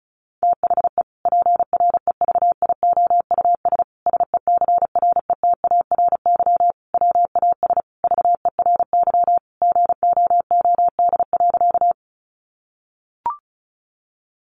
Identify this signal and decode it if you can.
Morse code, heard as a keyed tone, about 34 wpm, 700 Hz. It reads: THE PREVIOUS SECRETARY WAS VERY GOOD.
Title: Sentences from Top 1500 Words 35wpm